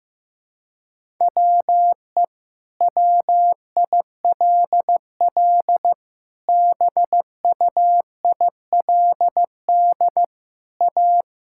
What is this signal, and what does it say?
Morse code, heard as a keyed tone, about 15 wpm, 700 Hz. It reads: WE WILL BUILD A
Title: Sentences from Top 1000 Words - Review Part 6 - 15wpm